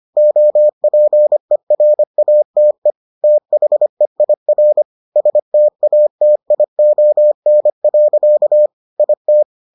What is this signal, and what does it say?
Morse code, heard as a keyed tone, about 25 wpm, 600 Hz. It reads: OPERATE THEIR STATION. IT